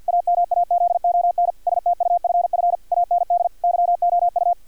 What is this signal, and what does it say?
Morse code, heard as a keyed tone, about 50 wpm, 700 Hz, with radio noise. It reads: AMAZON STUFF AND YOU
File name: archived / CRSnet